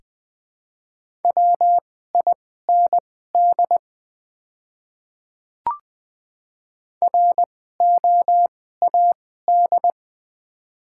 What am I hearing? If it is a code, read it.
Morse code, heard as a keyed tone, about 21 wpm, spaced as for 15 wpm, 690 Hz. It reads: WIND ROAD
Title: 2x Character Spacing - Top 500 Words - 20wpm